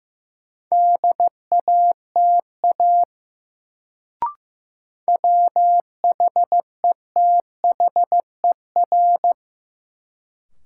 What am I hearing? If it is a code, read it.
Morse code, heard as a keyed tone, about 15 wpm, 700 Hz. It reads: DATA WHETHER
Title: Sentences from Top 1000 Words - Review Part 3 - New Word Drill - 15wpm